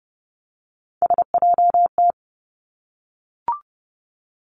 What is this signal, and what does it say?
Morse code, heard as a keyed tone, about 30 wpm, 700 Hz. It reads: SJT
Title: Three Challenging Letter-Number Combinations 30wpm